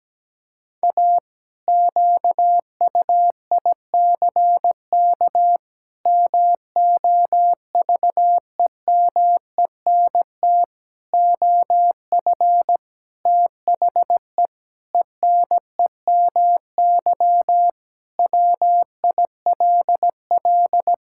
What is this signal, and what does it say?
Morse code, heard as a keyed tone, about 17 wpm, 700 Hz. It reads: A QUICK MOVEMENT OF THE ENEMY WILL
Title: Pangram Challenge v2 - 17wpm